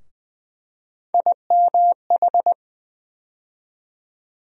Morse code, heard as a keyed tone, about 20 wpm, 700 Hz. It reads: IMH
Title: USA Call Sign Postfixes - 3 Letters 20wpm